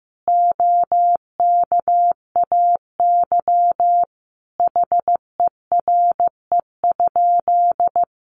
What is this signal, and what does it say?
Morse code, heard as a keyed tone, about 15 wpm, 700 Hz. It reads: OKAY HERE?
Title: A Sound Of Thunder 15wpm